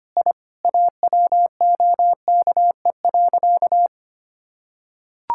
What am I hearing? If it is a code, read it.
Morse code, heard as a keyed tone, about 25 wpm, 700 Hz. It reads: I AWOKE.
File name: practice / 25wpm / Sentences with 4-Word Phrases - 25wpm